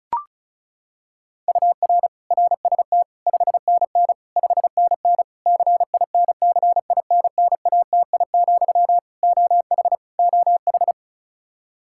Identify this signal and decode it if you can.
Morse code, heard as a keyed tone, about 36 wpm, 700 Hz. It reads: UR RST 5NN 5NN CINCINNATI, OH OH
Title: Practice QSO - Exchange 1 - Beginning - 35wpm